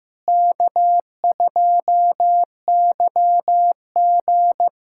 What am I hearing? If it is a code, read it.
Morse code, heard as a keyed tone, about 16 wpm, 700 Hz. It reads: K2YG
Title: K1USN SST QSO Practice - 15wpm